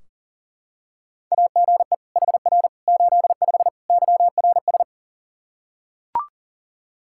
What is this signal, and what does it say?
Morse code, heard as a keyed tone, about 40 wpm, 700 Hz. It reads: AGE HR 85 YRS
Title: Practice QSO - Exchange 3 - More Info - 40wpm